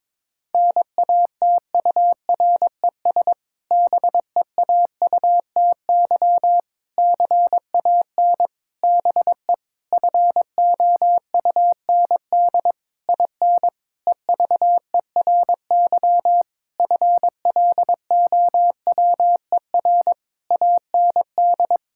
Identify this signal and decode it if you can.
Morse code, heard as a keyed tone, about 22 wpm, 700 Hz. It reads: NATURES BEAUTY CAN BE FOUND IN EVERY FLOWER AND
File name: Difficult Sentences with Hints - 22wpm